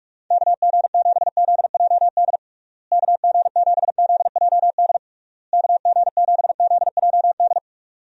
Morse code, heard as a keyed tone, about 42 wpm, 700 Hz. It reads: KG7ZJD KG7ZJD KG7ZJD
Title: USA Call Signs - 6 Characters - Repeated 3x - 45wpm